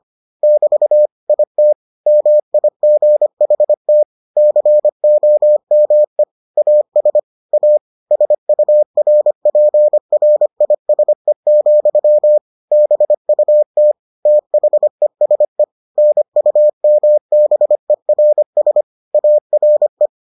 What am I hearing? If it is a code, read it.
Morse code, heard as a keyed tone, about 25 wpm, 600 Hz. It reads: <BT> IT MIGHT COME AS A SURPRISE, BUT THESE NUMBERS ARE